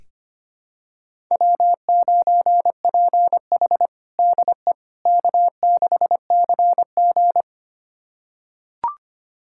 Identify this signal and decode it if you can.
Morse code, heard as a keyed tone, about 25 wpm, 700 Hz. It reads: W9PH DE K6CG
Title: USA Callsign DE Callsign - 4 Characters - 25wpm